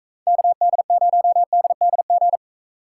Morse code, heard as a keyed tone, about 42 wpm, 700 Hz. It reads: KD0DDG